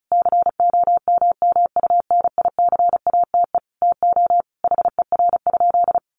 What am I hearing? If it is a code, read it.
Morse code, heard as a keyed tone, about 34 wpm, 700 Hz. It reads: COMMUNICATE TO HER?
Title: Sentences from Top 1300 Words 35wpm